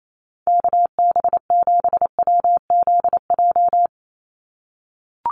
Morse code, heard as a keyed tone, about 28 wpm, 700 Hz. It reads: KB7WZJ
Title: USA Call Signs 28wpm